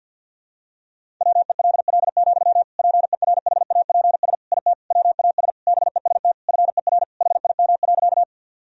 Morse code, heard as a keyed tone, about 52 wpm, 700 Hz. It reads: WELL, PERHAPS IT WAS BEST FER HIM.
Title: Sentences from Top 500 Words and Abbreviations 50wpm